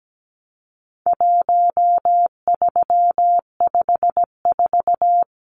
Morse code, heard as a keyed tone, about 17 wpm, 700 Hz. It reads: 1354
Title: Numbers - 4-Digit 17wpm